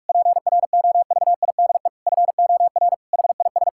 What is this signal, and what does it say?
Morse code, heard as a keyed tone, about 46 wpm, 700 Hz. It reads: PROVIDE FOR HIS
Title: Sentences from Top 1000 Words - Review Part 5 - 45wpm